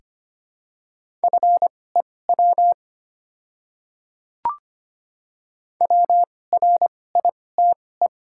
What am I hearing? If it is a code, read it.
Morse code, heard as a keyed tone, about 25 wpm, spaced as for 18 wpm, 690 Hz. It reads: FEW WRITE
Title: 2x Character Spacing - Top 300 Words - 25wpm